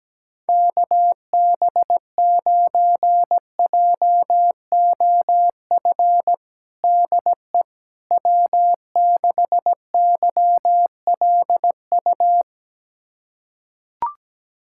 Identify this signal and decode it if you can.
Morse code, heard as a keyed tone, about 17 wpm, 700 Hz. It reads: KB9JOF DE W6YLU
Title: USA Callsign DE Callsign 17wpm